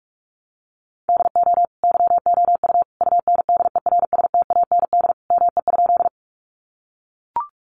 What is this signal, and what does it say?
Morse code, heard as a keyed tone, about 44 wpm, 700 Hz. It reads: DO YOU UNDERSTAND ME?
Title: Sentences from Top 600 Words 45wpm